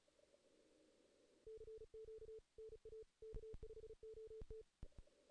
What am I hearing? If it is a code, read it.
Morse code, heard as a keyed tone, about 36 wpm, 440 Hz. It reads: CQ NA K5OT
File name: 2016 JAN CW NAQP / 2016 Jan naqp k5ot - 0-2